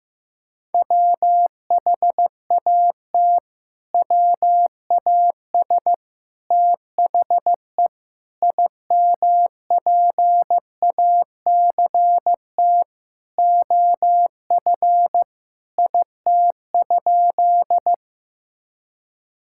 Morse code, heard as a keyed tone, about 15 wpm, 700 Hz. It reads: WHAT WAS THE IMPACT OF IT?
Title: Sentences from Top 1000 Words - Review Part 6 - 15wpm